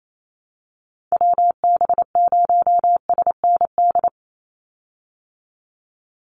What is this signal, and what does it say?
Morse code, heard as a keyed tone, about 28 wpm, 700 Hz. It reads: WB0SND
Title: Call Signs Repeated 3x - 28wpm